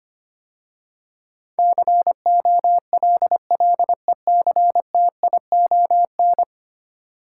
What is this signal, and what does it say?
Morse code, heard as a keyed tone, about 25 wpm, 700 Hz. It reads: COLLECTION
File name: 10 Letter Words - First 3 Letters Given - 25wpm